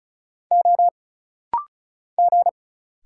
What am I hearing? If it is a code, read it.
Morse code, heard as a keyed tone, about 36 wpm, 700 Hz. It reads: O G
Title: Single Letter-Number Alt - Rapid-Fire 35wpm